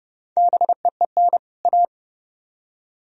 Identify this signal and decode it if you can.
Morse code, heard as a keyed tone, about 28 wpm, 700 Hz. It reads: BEEN A